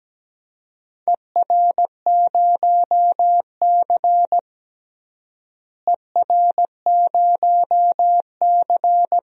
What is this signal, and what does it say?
Morse code, heard as a keyed tone, about 17 wpm, 700 Hz. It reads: ER0C ER0C
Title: DXCC-100 Call Signs - 4 Characters - Repeated 3x - 17wpm